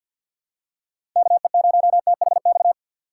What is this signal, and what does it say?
Morse code, heard as a keyed tone, about 53 wpm, 700 Hz. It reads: KE0THX
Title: USA Call Signs - 6 Characters - 6x Character Spacing - 50wpm